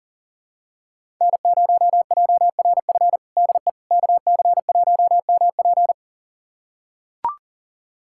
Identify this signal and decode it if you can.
Morse code, heard as a keyed tone, about 40 wpm, 700 Hz. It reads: N0JRF DE KC1MP
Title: USA Callsign DE Callsign - 5 Characters - 40wpm